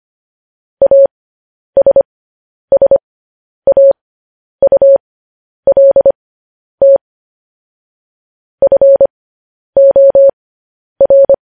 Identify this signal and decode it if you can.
Morse code, heard as a keyed tone, about 25 wpm, spaced as for 10 wpm, 550 Hz. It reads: ASSAULT FOR
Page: news-headlines-in-morse-code-at-10-wpm